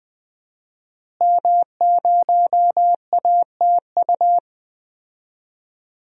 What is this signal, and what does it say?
Morse code, heard as a keyed tone, about 20 wpm, 700 Hz. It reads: M0ATU
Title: DXCC-100 Call Signs - 5 Characters - Repeated 3x - 20wpm